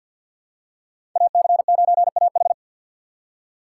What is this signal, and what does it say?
Morse code, heard as a keyed tone, about 52 wpm, 700 Hz. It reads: AC9AH
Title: USA Call Signs - 5 Characters - 3x Character Spacing - 50wpm